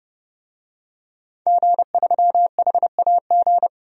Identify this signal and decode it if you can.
Morse code, heard as a keyed tone, about 30 wpm, 700 Hz. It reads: G3HAG
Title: DXCC-100 Call Signs - 5 Characters - 30wpm